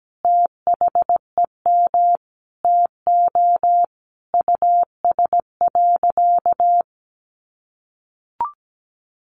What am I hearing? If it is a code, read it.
Morse code, heard as a keyed tone, about 17 wpm, 700 Hz. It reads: THEM TO US.
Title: Sentences from Top 1100 Words 17wpm